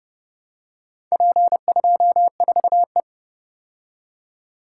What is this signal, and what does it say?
Morse code, heard as a keyed tone, about 30 wpm, 700 Hz. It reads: P24E